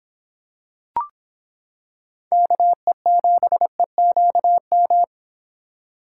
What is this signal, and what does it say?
Morse code, heard as a keyed tone, about 26 wpm, 700 Hz. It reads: KE7EQM